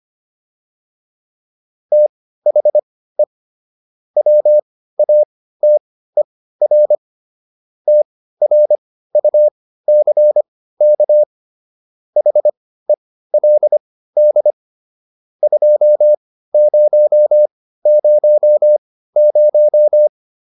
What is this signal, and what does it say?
Morse code, heard as a keyed tone, about 25 wpm, spaced as for 15 wpm, 600 Hz. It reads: THE WATER TRUCK HELD 2000